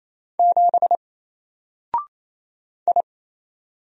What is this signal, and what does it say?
Morse code, heard as a keyed tone, about 27 wpm, 700 Hz. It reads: 7 I
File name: Single-Letter-Number-28wpm